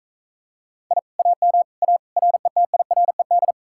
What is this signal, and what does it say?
Morse code, heard as a keyed tone, about 45 wpm, 700 Hz. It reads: I AM A RETIRED